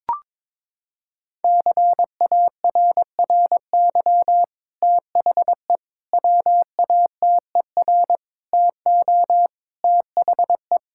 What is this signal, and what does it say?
Morse code, heard as a keyed tone, about 22 wpm, 700 Hz. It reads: CARRY THE WATER TO THE
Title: Sentences from Top 1600 Words 22wpm